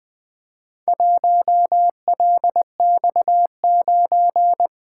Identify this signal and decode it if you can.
Morse code, heard as a keyed tone, about 20 wpm, 700 Hz. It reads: 1LX9
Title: Four Letter-Number Combinations - 20wpm